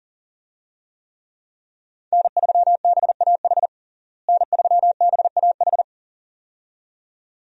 Morse code, heard as a keyed tone, about 40 wpm, 700 Hz. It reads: N3BAH N3BAH
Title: USA Call Signs - 5 Characters - Repeated 2x - 40wpm